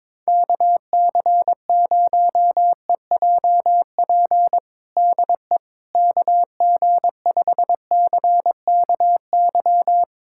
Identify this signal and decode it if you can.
Morse code, heard as a keyed tone, about 21 wpm, 700 Hz. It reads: KC0EJP DE KG5CKY
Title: USA Callsign DE Callsign - 6 Characters - 22wpm